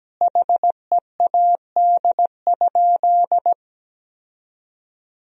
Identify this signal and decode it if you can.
Morse code, heard as a keyed tone, about 17 wpm, 700 Hz. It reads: HEAD?
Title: 3x Word-Spacing - Sentences from Top 300 Words - 17wpm